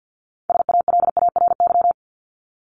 Morse code, heard as a keyed tone, about 49 wpm, 700 Hz. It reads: SALARY
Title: Top 1751-2000 Words 50wpm